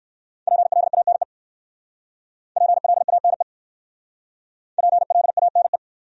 Morse code, heard as a keyed tone, about 58 wpm, 690 Hz. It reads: PLANE PLANE PLANE